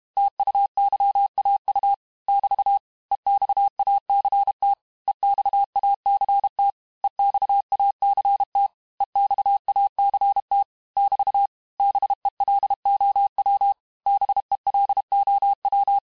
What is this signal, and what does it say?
Morse code, heard as a keyed tone, about 32 wpm, 800 Hz. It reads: TUYAU <BT> EXACT EXACT EXACT EXACT <BT> BELOW BELOW
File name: cours-du-mercredi / audios / cours-868-teste-MOTS-vitesse-30mpm